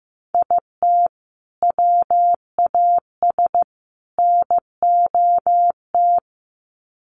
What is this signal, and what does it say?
Morse code, heard as a keyed tone, about 15 wpm, 700 Hz. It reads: IT WAS NOT
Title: Sets of 3 Words 15wpm